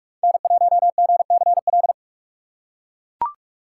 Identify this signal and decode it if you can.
Morse code, heard as a keyed tone, about 43 wpm, 700 Hz. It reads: N1GCL
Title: USA Call Signs - 5 Characters - 3x Character Spacing - 45wpm